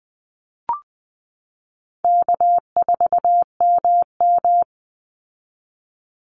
Morse code, heard as a keyed tone, about 20 wpm, 700 Hz. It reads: K4MM
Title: Easy Call Signs Repeated 2x - 20wpm